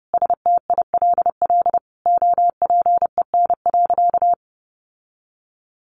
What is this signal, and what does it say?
Morse code, heard as a keyed tone, about 28 wpm, 700 Hz. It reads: STILL OPEN.
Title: Sentences from Top 1500 Words 30wpm